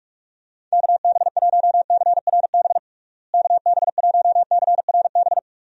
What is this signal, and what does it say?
Morse code, heard as a keyed tone, about 46 wpm, 700 Hz. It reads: KB1CRB KB1CRB
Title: USA Call Signs - 6 Characters - Repeated 2x - 45wpm